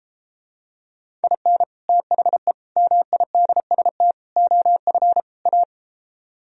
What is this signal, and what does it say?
Morse code, heard as a keyed tone, about 33 wpm, 700 Hz. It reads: IN THE MIDST OF A